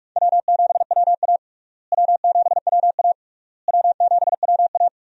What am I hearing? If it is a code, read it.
Morse code, heard as a keyed tone, about 46 wpm, 700 Hz. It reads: W7WA W7WA W7WA
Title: USA Call Signs - 4 Characters - Repeated 3x - 45wpm